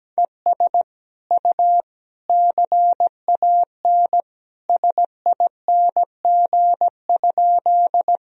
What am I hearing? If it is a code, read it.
Morse code, heard as a keyed tone, about 17 wpm, 700 Hz. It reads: ES U CAN SING?
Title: Sentences from Top 400 Words and Abbreviations 17wpm